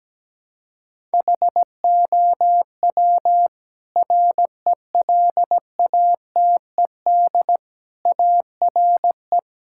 Morse code, heard as a keyed tone, about 17 wpm, 700 Hz. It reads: HOW RELATED ARE
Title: Sentences from Top 1000 Words - Review Part 5 - 17wpm